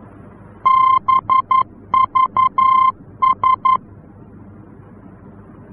Morse code, heard as a keyed tone, about 11 wpm, 1050 Hz. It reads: BVS
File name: france / mx / BVS-F-VOR